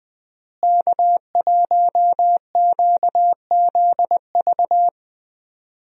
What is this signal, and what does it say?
Morse code, heard as a keyed tone, about 20 wpm, 700 Hz. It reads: K1QZV
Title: USA Call Signs - 5 Characters - Repeated 2x - 20wpm